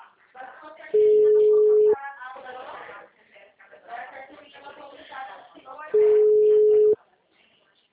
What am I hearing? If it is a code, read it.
Ringback tone (Europe, 425 Hz), with speech.
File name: out- -2015- - - .20809